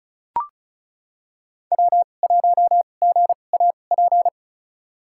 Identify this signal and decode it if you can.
Morse code, heard as a keyed tone, about 35 wpm, spaced as for 25 wpm, 690 Hz. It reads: W1GAP
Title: USA Call Signs - 5 Characters - 2x Character Spacing - 35wpm